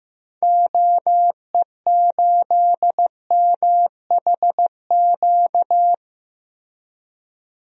Morse code, heard as a keyed tone, about 15 wpm, 700 Hz. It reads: OE8MHQ
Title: DXCC-100 Call Signs - 6 Characters - Repeated 2x - 15wpm